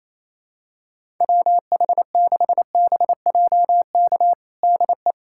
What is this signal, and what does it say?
Morse code, heard as a keyed tone, about 28 wpm, 700 Hz. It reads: WH6BJK DE
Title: USA Callsign DE Callsign 28wpm